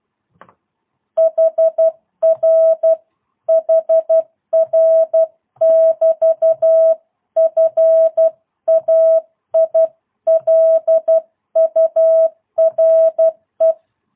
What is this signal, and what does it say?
Morse code, heard as a keyed tone, about 12 wpm, 650 Hz. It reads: HR HR <BT> FAILURE
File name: jh8cbh / 2022cw